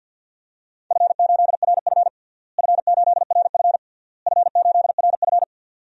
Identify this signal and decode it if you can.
Morse code, heard as a keyed tone, about 49 wpm, 700 Hz. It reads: F8RF F8RF F8RF
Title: DXCC-100 Call Signs - 4 Characters - Repeated 3x - 50wpm